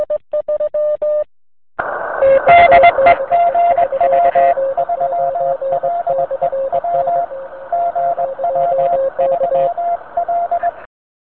Morse code, heard as a keyed tone, about 20 wpm, 690 Hz, with noise. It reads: DE GW3LEW GW3L